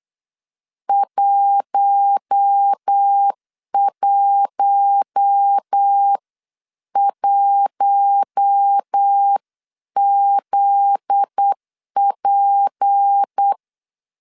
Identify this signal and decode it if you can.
Morse code, heard as a keyed tone, about 8 wpm, 800 Hz. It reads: 111ZP
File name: CW / DET / 1 9 1500C S B 8wpm 5624